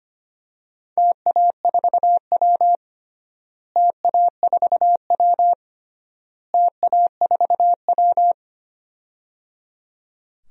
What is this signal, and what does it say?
Morse code, heard as a keyed tone, about 25 wpm, 700 Hz. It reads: TA4W TA4W TA4W
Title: DXCC-100 Call Signs - 4 Characters - Repeated 3x - 25wpm